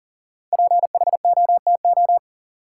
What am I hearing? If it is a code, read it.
Morse code, heard as a keyed tone, about 40 wpm, 700 Hz. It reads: PHOTO
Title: Sentences from Top 1200 Words - New Word Drill - 40wpm